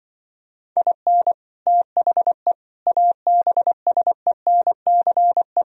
Morse code, heard as a keyed tone, about 24 wpm, 700 Hz. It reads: IN THE ABSENCE
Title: Speed-Racing - Sets of 4 Words 20wpm